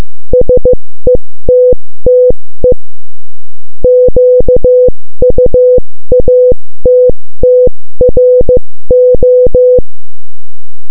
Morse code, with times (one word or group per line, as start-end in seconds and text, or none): 0.3-2.7 SETTE
3.8-9.8 QUATTRO